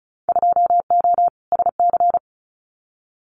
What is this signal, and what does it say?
Morse code, heard as a keyed tone, about 36 wpm, 700 Hz. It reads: 2O SC
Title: Winter Field Day Exchanges Encore 35wpm